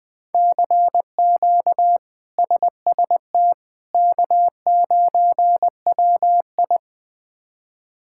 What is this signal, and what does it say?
Morse code, heard as a keyed tone, about 21 wpm, 700 Hz. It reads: CQ SST K9WI
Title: K1USN SST QSO Practice - 20wpm